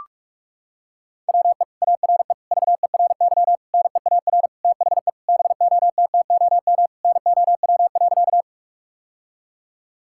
Morse code, heard as a keyed tone, about 46 wpm, 700 Hz. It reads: WE ARE VERY NEAR THE BOTTOM NOW.
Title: Sentences from Top 1000 Words - Review Part 6 - 45wpm